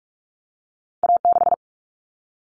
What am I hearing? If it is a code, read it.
Morse code, heard as a keyed tone, about 46 wpm, 700 Hz. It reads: A6
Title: Common Cut Numbers - 2-Digit 45wpm